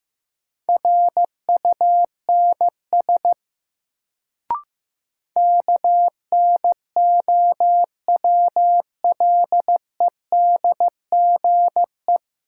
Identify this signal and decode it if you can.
Morse code, heard as a keyed tone, about 15 wpm, 700 Hz. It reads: RUNS KNOWLEDGE
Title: Sentences from Top 600 Words - New Word Drill - 15wpm